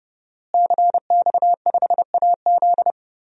Morse code, heard as a keyed tone, about 30 wpm, 700 Hz. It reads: CX5AZ
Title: DXCC-100 Call Signs - 5 Characters - Repeated 3x - 30wpm